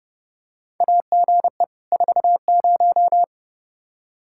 Morse code, heard as a keyed tone, about 30 wpm, 700 Hz. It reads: AGE 40
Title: K1USN SST QSO Practice 30wpm